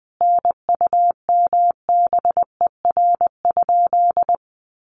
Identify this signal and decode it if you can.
Morse code, heard as a keyed tone, about 20 wpm, 700 Hz. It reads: NUMBER?
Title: Sentences from Top 100 Words 20wpm